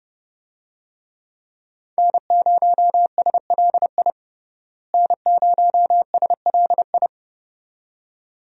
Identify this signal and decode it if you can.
Morse code, heard as a keyed tone, about 30 wpm, 700 Hz. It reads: N0SLI N0SLI
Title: USA Call Signs - 5 Characters - Repeated 2x - 30wpm